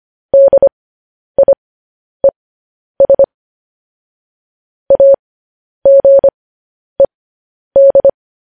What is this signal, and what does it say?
Morse code, heard as a keyed tone, about 25 wpm, spaced as for 10 wpm, 550 Hz. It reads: DIES AGED
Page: news headlines in morse code at 10 wpm